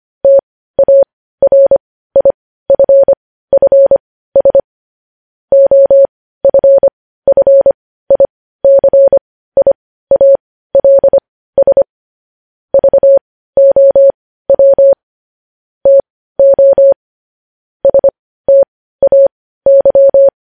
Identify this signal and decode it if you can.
Morse code, heard as a keyed tone, about 25 wpm, spaced as for 15 wpm, 550 Hz. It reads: TARIFFS OFFICIALS VOW TO STAY
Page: news-headlines-in-morse-code-at-15-wpm